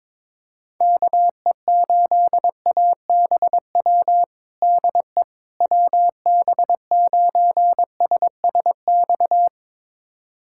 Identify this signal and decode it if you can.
Morse code, heard as a keyed tone, about 22 wpm, 700 Hz. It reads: KE8ABW DE WB9SSX
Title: USA Callsign DE Callsign - 6 Characters - 22wpm